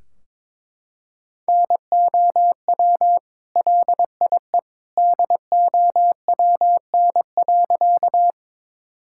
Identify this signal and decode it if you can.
Morse code, heard as a keyed tone, about 22 wpm, 700 Hz. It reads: NOW LIE DOWN.
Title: Sentences from Top 1600 Words 22wpm